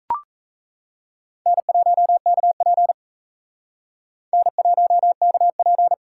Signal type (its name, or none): Morse code